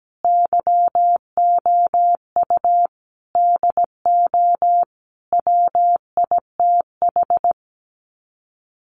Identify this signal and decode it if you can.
Morse code, heard as a keyed tone, about 17 wpm, 700 Hz. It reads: YOU DO WITH
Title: Sets of 5 Words Encore2 17wpm